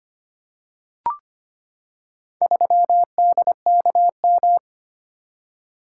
Morse code, heard as a keyed tone, about 25 wpm, 700 Hz. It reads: 3DKM
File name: practice / 25wpm / Four Letter-Number Combinations - 25wpm